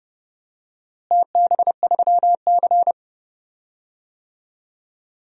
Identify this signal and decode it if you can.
Morse code, heard as a keyed tone, about 30 wpm, 700 Hz. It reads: TB3C
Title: DXCC-100 Call Signs - 4 Characters - Repeated 2x - 30wpm